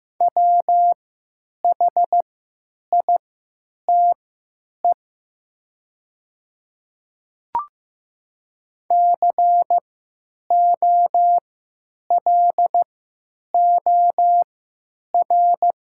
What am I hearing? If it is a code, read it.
Morse code, heard as a keyed tone, about 15 wpm, spaced as for 9 wpm, 700 Hz. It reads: WHITE COLOR
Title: 3x Character Spacing - Top 500 Words - 15wpm